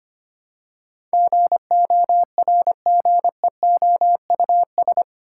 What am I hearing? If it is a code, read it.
Morse code, heard as a keyed tone, about 25 wpm, 700 Hz. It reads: GORGEOUS
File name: 8 Letter Words - First 3 Letters Given - 25wpm